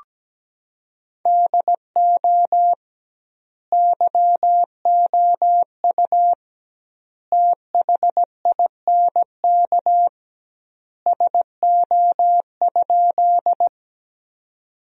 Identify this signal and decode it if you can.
Morse code, heard as a keyed tone, about 17 wpm, 700 Hz. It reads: DO YOU THINK SO?
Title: 2x Word-Spacing - Sentences from Top 200 Words - 17wpm